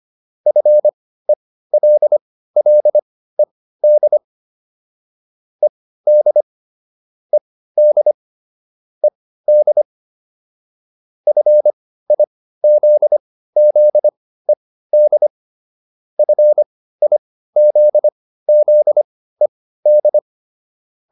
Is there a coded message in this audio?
Yes: Morse code from 0.5 to 20.2 s, heard as a keyed tone, about 25 wpm, spaced as for 15 wpm, 600 Hz. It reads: FELLED ED ED ED FIZZED FIZZED